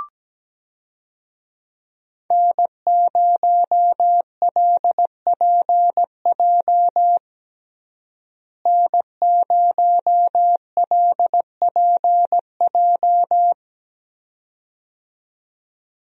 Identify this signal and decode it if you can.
Morse code, heard as a keyed tone, about 17 wpm, 700 Hz. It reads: N0LPJ N0LPJ
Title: USA Call Signs - 5 Characters - Repeated 2x - 17wpm